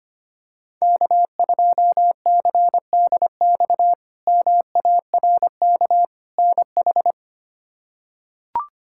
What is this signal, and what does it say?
Morse code, heard as a keyed tone, about 25 wpm, 700 Hz. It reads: K2CDX MARK NH
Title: K1USN SST QSO Practice - 25wpm